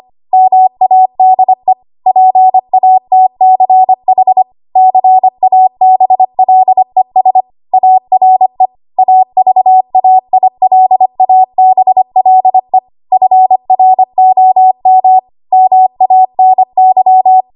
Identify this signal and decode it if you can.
Morse code, heard as a keyed tone, about 25 wpm, 750 Hz. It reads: MADE PATCH CABLES ARE AVAILABLE FROM MANY
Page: W1AW Web Code Practice Files